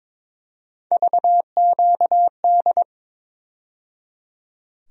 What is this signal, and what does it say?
Morse code, heard as a keyed tone, about 22 wpm, 700 Hz. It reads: VQD
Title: USA Call Sign Postfixes - 3 Letters 22wpm